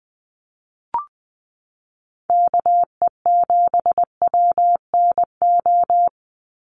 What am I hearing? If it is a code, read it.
Morse code, heard as a keyed tone, about 20 wpm, 700 Hz. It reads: KE7WNO
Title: USA Call Signs 20wpm